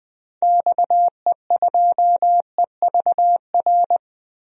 Morse code, heard as a keyed tone, about 20 wpm, 700 Hz. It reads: XE2EVR
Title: DXCC-100 Call Signs - 6 Characters - 20wpm